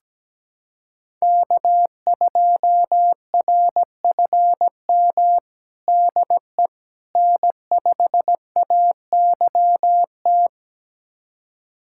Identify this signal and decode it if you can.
Morse code, heard as a keyed tone, about 17 wpm, 700 Hz. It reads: K2RFM DE N5AYT
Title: USA Callsign DE Callsign - 5 Characters - 17wpm